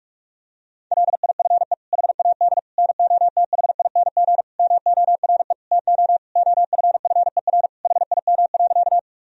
Morse code, heard as a keyed tone, about 46 wpm, 700 Hz. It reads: LIFE HAD NOTHING MORE TO OFFER HIM.
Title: Sentences from Top 1000 Words - Review Part 4 - 45wpm